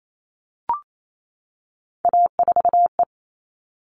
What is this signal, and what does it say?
Morse code, heard as a keyed tone, about 28 wpm, 700 Hz. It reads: A4E
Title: All Cut Numbers - 3-Digit 28wpm